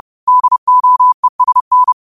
Morse code, heard as a keyed tone, about 30 wpm, 1000 Hz. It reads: DOESN